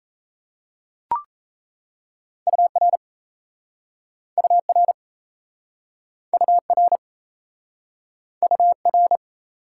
Morse code, heard as a keyed tone, about 38 wpm, 700 Hz. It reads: UR UR UR UR